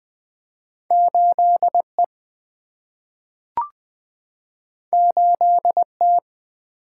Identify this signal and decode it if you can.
Morse code, heard as a keyed tone, about 20 wpm, 700 Hz. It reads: 8E 8T
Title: Single-Letter Single-Number 20wpm